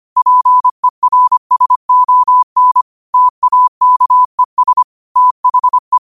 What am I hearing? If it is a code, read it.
Morse code, heard as a keyed tone, about 25 wpm, 1000 Hz. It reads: PERSON TAKES THE
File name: qotdfiles / 25wpm